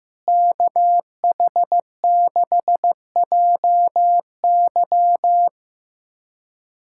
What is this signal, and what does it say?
Morse code, heard as a keyed tone, about 15 wpm, 700 Hz. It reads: KH6JY
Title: USA Callsign DE Callsign - 5 Characters - 15wpm